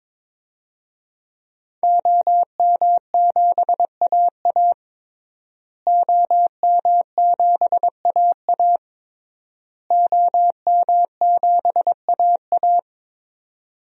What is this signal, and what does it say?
Morse code, heard as a keyed tone, about 22 wpm, 700 Hz. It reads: OM7AA OM7AA OM7AA
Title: DXCC-100 Call Signs - 5 Characters - Repeated 3x - 22wpm